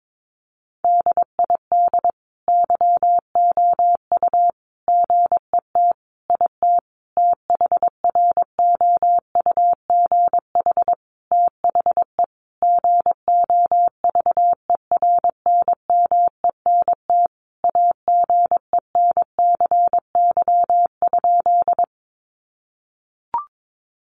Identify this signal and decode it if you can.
Morse code, heard as a keyed tone, about 22 wpm, 700 Hz. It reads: DID YOU GET IT THROUGH THE GOVERNMENT AGENCY?
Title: Sentences from Top 1300 Words 22wpm